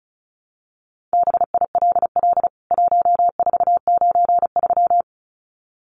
Morse code, heard as a keyed tone, about 34 wpm, 700 Hz. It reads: BILL 1493
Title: CWT Exchange 35wpm